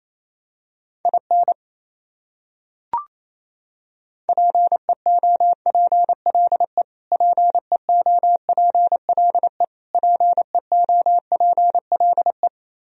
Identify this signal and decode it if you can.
Morse code, heard as a keyed tone, about 28 wpm, 700 Hz. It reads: IN PEOPLE PEOPLE PEOPLE
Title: Top 100 Words Repeated 28wpm